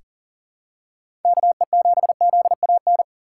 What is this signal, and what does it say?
Morse code, heard as a keyed tone, about 40 wpm, 700 Hz. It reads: KE7ZAN
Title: USA Call Signs - 6x Character Spacing - 40wpm